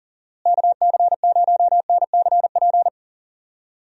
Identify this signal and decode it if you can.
Morse code, heard as a keyed tone, about 40 wpm, 700 Hz. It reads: KC0NCP
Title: USA Call Signs - 6 Characters - Repeated 3x - 40wpm